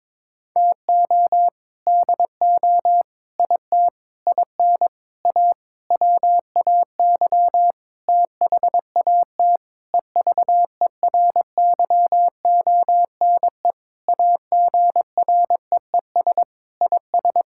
Morse code, heard as a keyed tone, about 22 wpm, 700 Hz. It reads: TO DO IT IN A WAY THAT EVERYONE AGREES IS
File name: Sentences with 7-Word Phrases - 22wpm